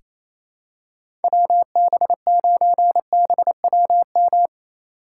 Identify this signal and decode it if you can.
Morse code, heard as a keyed tone, about 28 wpm, 700 Hz. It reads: WB9BWM
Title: USA Call Signs - 6 Characters - 4x Character Spacing - 28wpm